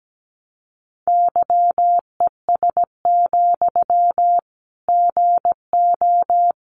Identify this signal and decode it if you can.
Morse code, heard as a keyed tone, about 17 wpm, 700 Hz. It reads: YES, GO
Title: Sentences from Top 500 Words 17wpm